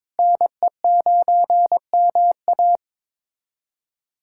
Morse code, heard as a keyed tone, about 21 wpm, 700 Hz. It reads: NE9MA
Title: USA Callsign DE Callsign - 5 Characters - 22wpm